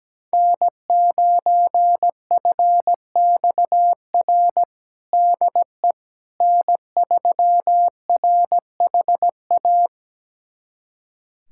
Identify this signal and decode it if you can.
Morse code, heard as a keyed tone, about 17 wpm, 700 Hz. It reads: N9FXR DE N3RHA
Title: USA Callsign DE Callsign - 5 Characters - 17wpm